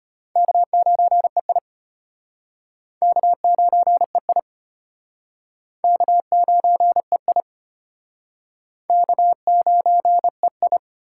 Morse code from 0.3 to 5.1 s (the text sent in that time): K9EI K9EI